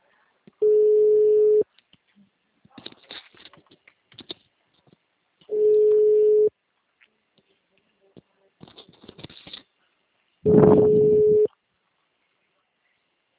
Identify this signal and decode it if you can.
Ringback tone (Europe, 425 Hz).